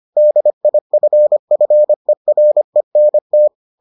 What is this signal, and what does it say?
Morse code, heard as a keyed tone, about 25 wpm, 600 Hz. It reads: DIFFERENT